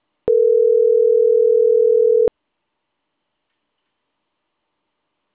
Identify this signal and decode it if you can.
Ringback tone (North America), 440 + 480 Hz.